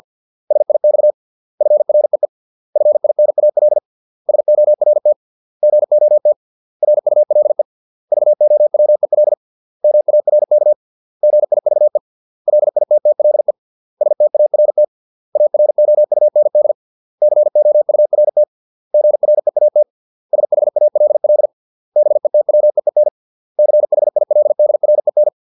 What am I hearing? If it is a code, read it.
Morse code, heard as a keyed tone, about 50 wpm, 600 Hz. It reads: SIX FREE FINAL SORT GOT RULE VOWEL MARK GIVE LITTLE START AROUND COURT GREAT SHALL BETWEEN CHILDREN